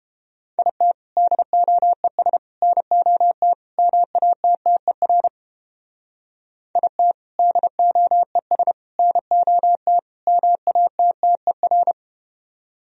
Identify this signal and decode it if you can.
Morse code, heard as a keyed tone, about 31 wpm, 700 Hz. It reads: IT DOES NOT MATTER IT DOES NOT MATTER